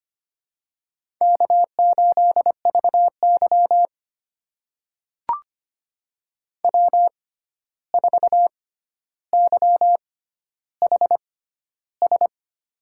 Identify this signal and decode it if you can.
Morse code, heard as a keyed tone, about 25 wpm, spaced as for 12 wpm, 700 Hz. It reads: K8VY W4YHS
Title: USA Call Signs - 6x Character Spacing - 25wpm